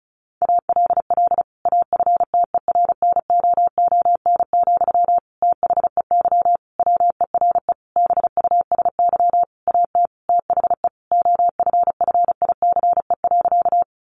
Morse code, heard as a keyed tone, about 34 wpm, 700 Hz. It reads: ALL AFTERNOON, THEY WERE BUSY AT THE OFFICE.
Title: Sentences from Top 1500 Words 35wpm